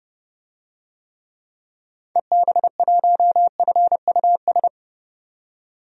Morse code, heard as a keyed tone, about 29 wpm, 700 Hz. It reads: EB1FUS
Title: DXCC-100 Call Signs - 6 Characters - 30wpm